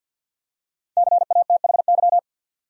Morse code, heard as a keyed tone, about 52 wpm, 700 Hz. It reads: CATHY
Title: USA Common Names - 50wpm